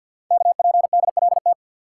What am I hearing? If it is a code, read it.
Morse code, heard as a keyed tone, about 48 wpm, 700 Hz. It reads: KPDLT
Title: Code Groups - 5 Letters - 50wpm